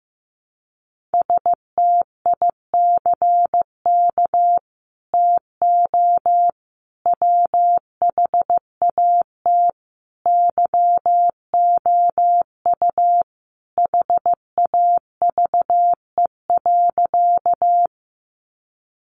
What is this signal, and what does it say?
Morse code, heard as a keyed tone, about 15 wpm, 700 Hz. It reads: STICK TO WHAT YOU HAVE.
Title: Sentences from Top 1200 Words 15wpm